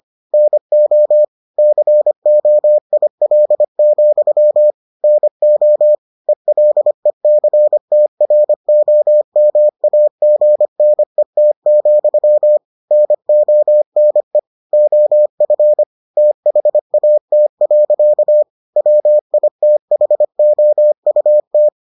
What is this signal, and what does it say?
Morse code, heard as a keyed tone, about 25 wpm, 600 Hz. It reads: NO COIL, NO ELECTROMAGNET, NONE OF THAT. WITHOUT